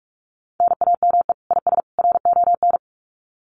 Morse code, heard as a keyed tone, about 44 wpm, 700 Hz. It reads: NAME IS RON
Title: Common QSO Phrases 45wpm